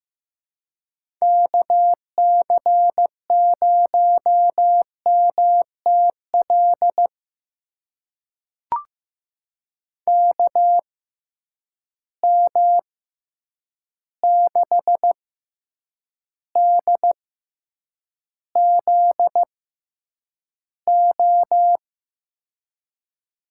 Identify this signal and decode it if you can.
Morse code, heard as a keyed tone, about 15 wpm, spaced as for 8 wpm, 700 Hz. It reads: KC0MTL KM6DZO
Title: USA Call Signs - 6 Characters - 6x Character Spacing - 15wpm